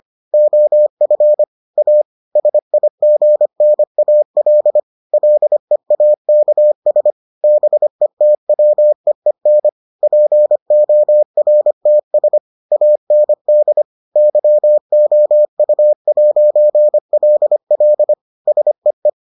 Morse code, heard as a keyed tone, about 25 wpm, 600 Hz. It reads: OF A SIGNAL LEAKS BETWEEN PORTS AND YOU'LL SEE